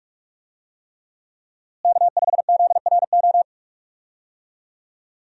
Morse code, heard as a keyed tone, about 43 wpm, 700 Hz. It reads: K5ZRO
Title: USA Call Signs - 5 Characters - Repeated 3x - 45wpm